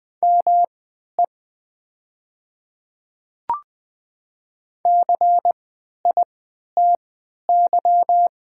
Morse code, heard as a keyed tone, about 20 wpm, spaced as for 11 wpm, 700 Hz. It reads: ME CITY
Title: 3x Character Spacing - Top 300 Words - 20wpm